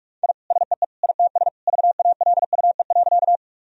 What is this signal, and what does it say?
Morse code, heard as a keyed tone, about 46 wpm, 700 Hz. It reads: I SEE ITS VALUE.
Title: Sentences from Top 1000 Words - Review Part 3 - 45wpm